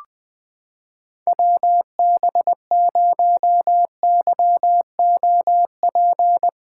Morse code, heard as a keyed tone, about 20 wpm, 700 Hz. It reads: WB0YOP
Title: USA Callsign DE Callsign 20wpm